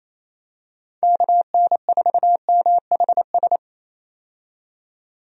Morse code, heard as a keyed tone, about 27 wpm, 700 Hz. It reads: KN4MHS
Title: USA Call Signs - 6 Characters - 28wpm